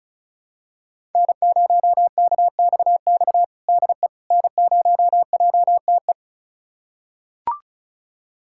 Morse code, heard as a keyed tone, about 36 wpm, 700 Hz. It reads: N0KXX DE N0JTE
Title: USA Callsign DE Callsign - 5 Characters - 35wpm